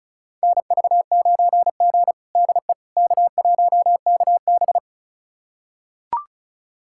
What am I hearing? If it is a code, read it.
Morse code, heard as a keyed tone, about 36 wpm, 700 Hz. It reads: NV9G DE K1KB
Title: USA Callsign DE Callsign - 4 Characters - 35wpm